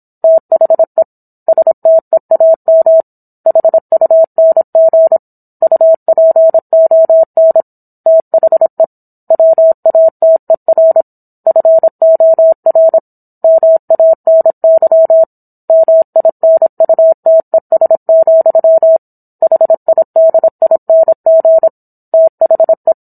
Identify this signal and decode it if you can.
Morse code, heard as a keyed tone, about 27 wpm, 650 Hz. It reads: THE STEAM HUNG UPON THE WATER FOR MANY MINUTES, HIDING THE